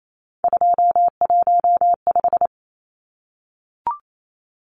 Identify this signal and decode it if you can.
Morse code, heard as a keyed tone, about 28 wpm, 700 Hz. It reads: 215
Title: Numbers - 3-Digit 28wpm